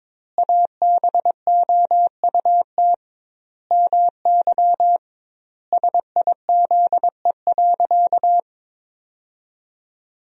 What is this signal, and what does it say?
Morse code, heard as a keyed tone, about 22 wpm, 700 Hz. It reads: ABOUT MY SIZE.
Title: 2x Word-Spacing - Sentences from Top 400 Words - 22wpm